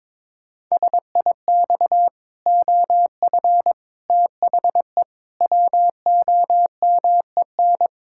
Morse code, heard as a keyed tone, about 22 wpm, 700 Hz. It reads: SIX OF THE WOMEN
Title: Pangram Challenge v2 - 22wpm